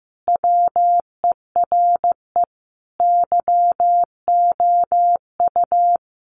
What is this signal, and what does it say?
Morse code, heard as a keyed tone, about 15 wpm, 700 Hz. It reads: WERE YOU
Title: Sentences from Top 200 Words 15wpm